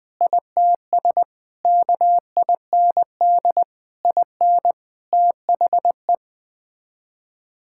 Morse code, heard as a keyed tone, about 20 wpm, 700 Hz. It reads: ITS KIND IN THE
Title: Sets of 8 Words - 20wpm